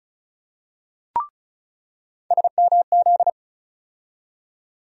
Morse code, heard as a keyed tone, about 36 wpm, 700 Hz. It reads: SMZ
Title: USA Call Sign Postfixes - 3 Letters 35wpm